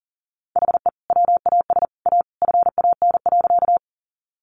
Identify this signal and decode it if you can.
Morse code, heard as a keyed tone, about 40 wpm, 700 Hz. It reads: HE WAS A FAN.
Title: Sentences from Top 1300 Words 40wpm